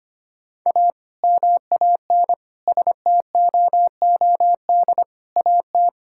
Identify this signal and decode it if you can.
Morse code, heard as a keyed tone, about 25 wpm, 700 Hz. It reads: A MAN STOOD AT
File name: practice / 25wpm / Sentences with 2-Word Phrases - 25wpm